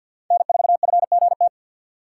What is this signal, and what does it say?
Morse code, heard as a keyed tone, about 62 wpm, 700 Hz. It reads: N4FGT
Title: Speed Racing - USA 1x3 Call Signs 50wpm